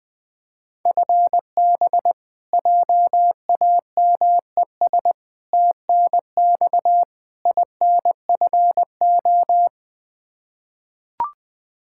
Morse code, heard as a keyed tone, about 20 wpm, 700 Hz. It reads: FB JAMES TNX INFO
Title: Practice QSO - Exchange 3 - More Info - 20wpm